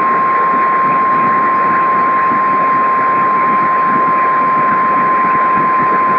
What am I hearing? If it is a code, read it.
Emergency Alert System (SAME) tones (the 1050 Hz attention tone), with noise. It is heard from 0.6 to 5.7 s.